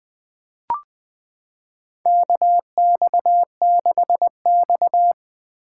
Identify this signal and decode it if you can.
Morse code, heard as a keyed tone, about 20 wpm, 700 Hz. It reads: KX6X
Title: K1USN SST QSO Practice - 20wpm